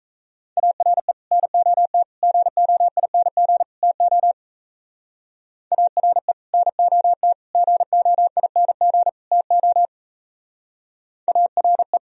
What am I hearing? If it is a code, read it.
Morse code, heard as a keyed tone, about 34 wpm, 700 Hz. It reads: ARE NOT GOING TO ARE NOT GOING TO ARE
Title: Speed-Racing - Sets of 4 Words 28wpm